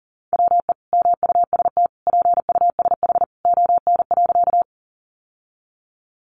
Morse code, heard as a keyed tone, about 40 wpm, 700 Hz. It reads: WE MUST PUSH ON.
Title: Sentences from Top 1300 Words 40wpm